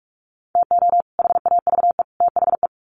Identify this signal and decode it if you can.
Morse code, heard as a keyed tone, about 44 wpm, 700 Hz. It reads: TO HAVE THE
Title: Sets of 3 Words 45wpm